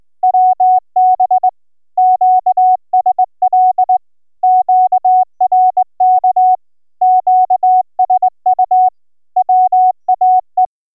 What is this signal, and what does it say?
Morse code, heard as a keyed tone, about 22 wpm, 740 Hz. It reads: WB QSL QRK QSU WAE